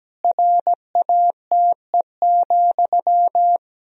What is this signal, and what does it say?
Morse code, heard as a keyed tone, about 17 wpm, 700 Hz. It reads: RATE,
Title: Sentences from Top 1000 Words - Review Part 4 - 17wpm